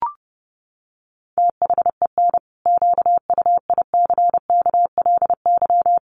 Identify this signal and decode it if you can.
Morse code, heard as a keyed tone, about 30 wpm, 700 Hz. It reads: THEN QUICKLY
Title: Sentences from Top 700 Words 30wpm